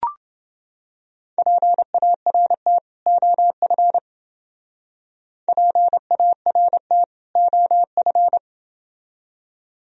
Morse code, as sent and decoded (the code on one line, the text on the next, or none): .--. .- .-. - / --- ..-. / .--. .- .-. - / --- ..-.
PART OF PART OF